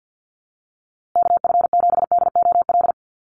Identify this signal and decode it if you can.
Morse code, heard as a keyed tone, about 50 wpm, 700 Hz. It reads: KF7DOL
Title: USA Call Signs Repeated 2x 50wpm